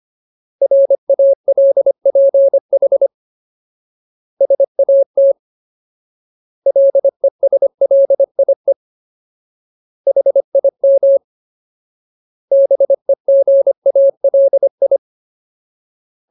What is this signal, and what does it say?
Morse code, heard as a keyed tone, about 25 wpm, 550 Hz. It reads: RALPH SAT LESLIE HIM BEGALI
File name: Ultimate Word List / 25WPM Ultimate